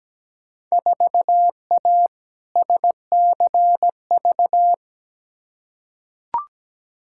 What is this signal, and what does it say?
Morse code, heard as a keyed tone, about 17 wpm, 700 Hz. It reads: 4A SCV
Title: Field Day Exchanges - 17wpm